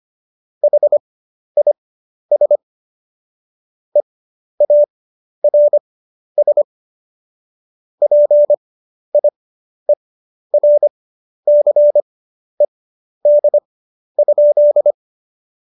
Morse code, heard as a keyed tone, about 25 wpm, spaced as for 11 wpm, 590 Hz. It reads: HIS EARS PIERCED?
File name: Jokes-25-char-10-eff